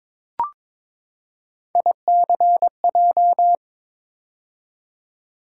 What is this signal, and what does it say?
Morse code, heard as a keyed tone, about 22 wpm, 700 Hz. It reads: ICJ
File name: practice / 22wpm / Three Letter-Number Combinations - 22wpm